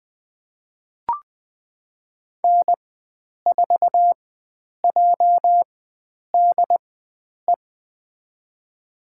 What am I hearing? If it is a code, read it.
Morse code, heard as a keyed tone, about 20 wpm, spaced as for 9 wpm, 700 Hz. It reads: N4JDE
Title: USA Call Signs - 5 Characters - 4x Character Spacing - 20wpm